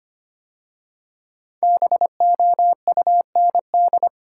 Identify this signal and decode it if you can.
Morse code, heard as a keyed tone, about 25 wpm, 700 Hz. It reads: BOUND
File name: 5 Letter Words - First 3 Letters Given - 25wpm